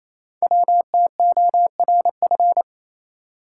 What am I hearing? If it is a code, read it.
Morse code, heard as a keyed tone, about 28 wpm, 700 Hz. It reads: WTORF
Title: Code Groups - 5 Letter:Numbers - 28wpm